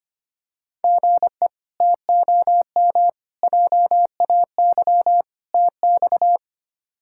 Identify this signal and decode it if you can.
Morse code, heard as a keyed tone, about 25 wpm, 700 Hz. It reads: GE TOM JAY TX
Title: K1USN SST QSO Practice - 25wpm